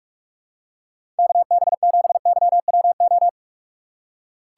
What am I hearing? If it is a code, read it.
Morse code, heard as a keyed tone, about 45 wpm, 700 Hz. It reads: KB7YWO
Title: USA Call Signs - 4x Character Spacing - 45wpm